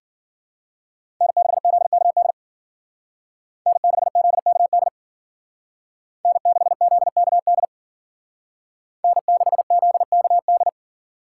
Morse code, heard as a keyed tone, about 52 wpm, 700 Hz. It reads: N6ZKD N6ZKD N6ZKD N6ZKD